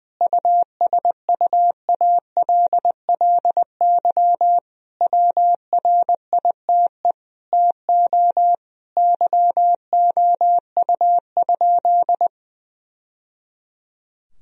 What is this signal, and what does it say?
Morse code, heard as a keyed tone, about 21 wpm, 700 Hz. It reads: USUALLY WRITE TO YOU?
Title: Sentences from Top 1000 Words - Review Part 3 - 20wpm